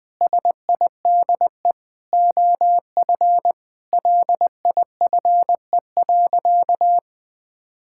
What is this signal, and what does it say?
Morse code, heard as a keyed tone, about 21 wpm, 700 Hz. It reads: SIDE OF LIFE.
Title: Sentences with 8-Word Phrases - 20wpm